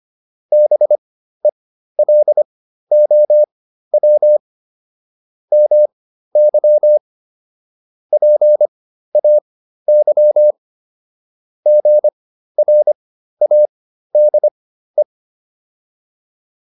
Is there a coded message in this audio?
Yes: Morse code from 0.5 to 15.0 s, heard as a keyed tone, about 25 wpm, spaced as for 13 wpm, 600 Hz. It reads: BELOW MY PAY GRADE